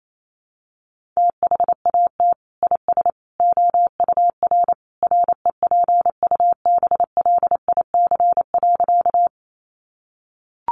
Morse code, heard as a keyed tone, about 28 wpm, 700 Hz. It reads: THAT IS OUR REPUBLIC.
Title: Sentences from Top 1500 Words 28wpm